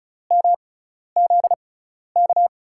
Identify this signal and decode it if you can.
Morse code, heard as a keyed tone, about 36 wpm, spaced as for 12 wpm, 700 Hz. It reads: MZK
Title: USA Call Signs - 5 Characters - 6x Character Spacing - 35wpm